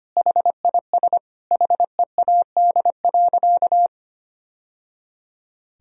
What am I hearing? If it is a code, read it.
Morse code, heard as a keyed tone, about 25 wpm, 700 Hz. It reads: HIS HEAD.
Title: Sentences from Top 1000 Words - Review Part 1 - 25wpm